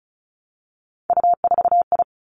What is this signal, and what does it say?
Morse code, heard as a keyed tone, about 36 wpm, 700 Hz. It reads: U4I
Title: Three Challenging Letter-Number Combinations 35wpm